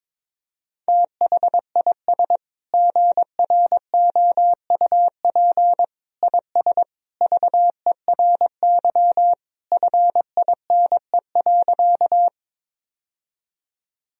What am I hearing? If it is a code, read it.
Morse code, heard as a keyed tone, about 22 wpm, 700 Hz. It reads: THIS GROUP IS VERY FINE.
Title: Sentences from Top 1000 Words - Review Part 1 - 22wpm